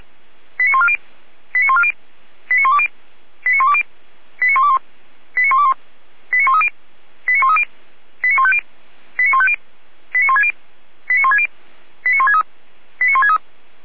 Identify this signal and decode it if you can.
Five-tone selective call (ZVEI), heard as quick runs of five tones, tones of about 70 ms. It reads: 89140 89140 89120 89120 89121 89121 89130 89130 89150 89150 89160 89160 89163 89163